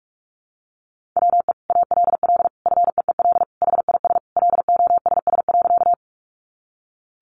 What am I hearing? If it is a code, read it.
Morse code, heard as a keyed tone, about 44 wpm, 700 Hz. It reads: WE ALL FEEL HIS LOSS.
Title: Sentences from Top 1400 Words 45wpm